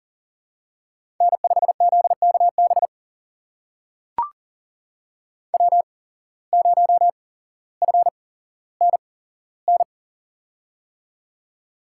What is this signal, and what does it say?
Morse code, heard as a keyed tone, about 40 wpm, spaced as for 17 wpm, 700 Hz. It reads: N5ZKB W0FNN
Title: USA Call Signs - 5 Characters - 8x Character Spacing - 40wpm